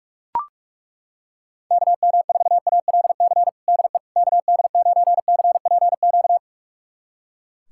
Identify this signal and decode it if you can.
Morse code, heard as a keyed tone, about 41 wpm, 700 Hz. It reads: KM4ALC DE KD9CPQ
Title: USA Callsign DE Callsign 45wpm